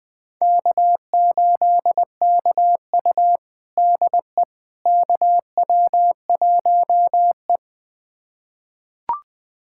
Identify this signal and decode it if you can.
Morse code, heard as a keyed tone, about 20 wpm, 700 Hz. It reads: K8KU DE KW1E
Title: USA Callsign DE Callsign - 4 Characters - 20wpm